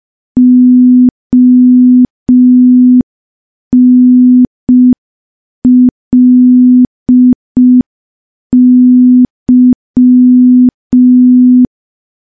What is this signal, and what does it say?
Morse code, heard as a keyed tone, about 5 wpm, 250 Hz. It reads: ONLY
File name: morse / morse-training-05WPM